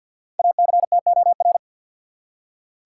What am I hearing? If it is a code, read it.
Morse code, heard as a keyed tone, about 49 wpm, 700 Hz. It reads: ACTOR
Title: Five Letter Words - 50wpm